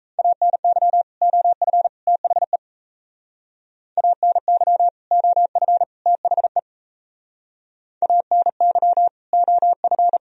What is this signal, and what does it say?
Morse code, heard as a keyed tone, about 35 wpm, 700 Hz. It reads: ANY OF THE ANY OF THE ANY OF
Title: Speed-Racing - Sets of 3 Words 28wpm